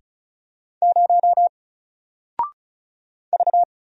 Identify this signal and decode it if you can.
Morse code, heard as a keyed tone, about 36 wpm, 700 Hz. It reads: 0 V
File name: Single-Letter-Number-35wpm